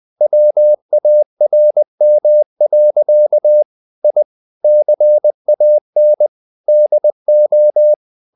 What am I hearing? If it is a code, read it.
Morse code, heard as a keyed tone, about 20 wpm, 600 Hz. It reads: WARM. I CAN DO
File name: the old man and the sea / the old man and the sea part 1 20wpm